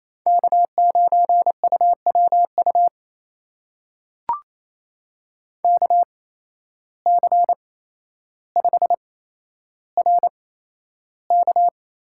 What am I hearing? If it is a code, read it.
Morse code, heard as a keyed tone, about 28 wpm, spaced as for 12 wpm, 700 Hz. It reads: K9UWU KC5RK
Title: USA Call Signs - 5 Characters - 8x Character Spacing - 28wpm